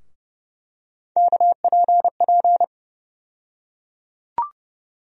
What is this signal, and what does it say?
Morse code, heard as a keyed tone, about 30 wpm, 700 Hz. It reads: KPP
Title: USA Call Sign Postfixes - 3 Letters 30wpm